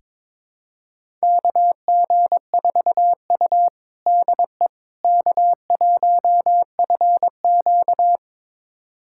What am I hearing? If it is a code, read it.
Morse code, heard as a keyed tone, about 22 wpm, 700 Hz. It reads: KG4U DE K1FQ
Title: USA Callsign DE Callsign - 4 Characters - 22wpm